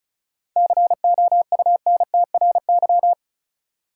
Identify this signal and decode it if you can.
Morse code, heard as a keyed tone, about 35 wpm, 700 Hz. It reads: COUNTRY
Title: Sentences from Top 300 Words - New Word Drill - 35wpm